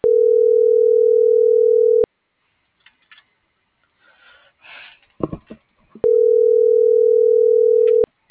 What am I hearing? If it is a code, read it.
Ringback tone (North America), 440 + 480 Hz.